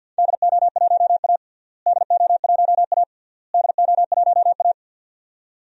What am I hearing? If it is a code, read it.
Morse code, heard as a keyed tone, about 49 wpm, 700 Hz. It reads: DO1A DO1A DO1A
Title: DXCC-100 Call Signs - 4 Characters - Repeated 3x - 50wpm